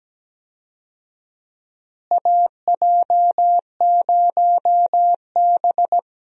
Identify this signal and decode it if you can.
Morse code, heard as a keyed tone, about 17 wpm, 700 Hz. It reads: AJ0B
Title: USA Call Signs - 4 Characters - Repeated 3x - 17wpm